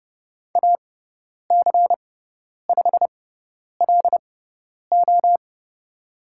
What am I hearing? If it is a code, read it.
Morse code, heard as a keyed tone, about 30 wpm, spaced as for 10 wpm, 700 Hz. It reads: AC5LO
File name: Lesson 16g - Letter c - Call Signs 30wpm